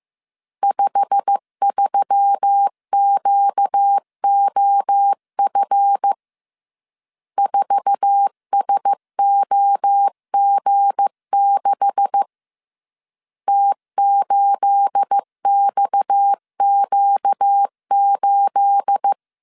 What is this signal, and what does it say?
Morse code, heard as a keyed tone, about 15 wpm, 800 Hz. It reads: 53QOF 4SOG6 T8XQ8
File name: CW / DET / 1 9 1500C S C 14wpm 3245